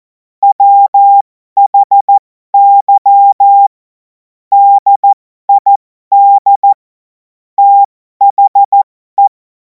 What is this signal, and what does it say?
Morse code, heard as a keyed tone, about 14 wpm, spaced as for 12 wpm, 800 Hz. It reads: WHY DID THE
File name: Morsekurs / Witze